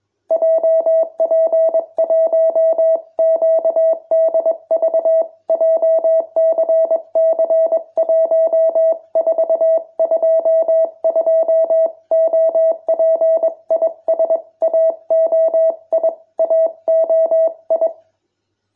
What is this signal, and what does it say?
Morse code, heard as a keyed tone, about 21 wpm, 650 Hz. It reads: JP1QDV JCC1422 OP IS AOI AOI